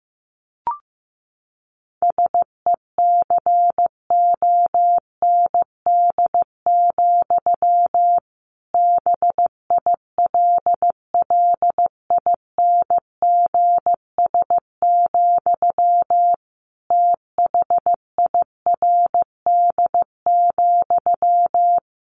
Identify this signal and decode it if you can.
Morse code, heard as a keyed tone, about 15 wpm, 700 Hz. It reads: SECOND, BILLINGS, THIRD,
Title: A Sound Of Thunder 15wpm